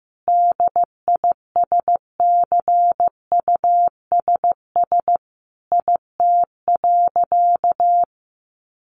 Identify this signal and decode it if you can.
Morse code, heard as a keyed tone, about 15 wpm, 700 Hz. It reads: DISCUSS IT.
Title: Sentences from Top 1400 Words 15wpm